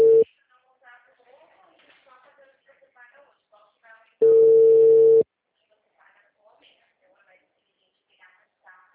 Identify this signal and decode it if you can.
Ringback tone (China, 450 Hz), with speech.